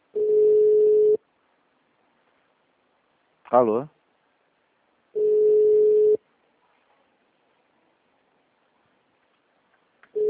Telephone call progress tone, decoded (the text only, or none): ringback tone